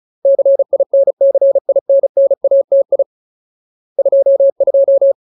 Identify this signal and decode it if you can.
Morse code, heard as a keyed tone, about 34 wpm, 550 Hz. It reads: CINCINNATI 22
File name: Ultimate Word List / 35WPM Ultimate